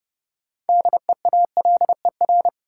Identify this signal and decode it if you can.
Morse code, heard as a keyed tone, about 30 wpm, 700 Hz. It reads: DEALER
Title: Sentences from Top 1100 Words - New Word Drill Alt - 30wpm